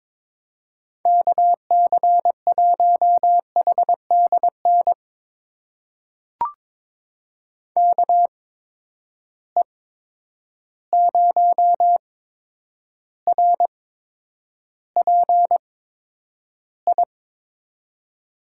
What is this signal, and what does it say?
Morse code, heard as a keyed tone, about 22 wpm, spaced as for 9 wpm, 700 Hz. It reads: KC1HDN KE0RPI
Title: USA Call Signs - 8x Character Spacing - 22wpm